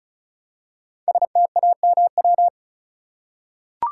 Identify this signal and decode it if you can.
Morse code, heard as a keyed tone, about 36 wpm, 700 Hz. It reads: STAMW
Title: Code Groups - 5 Letters - 35wpm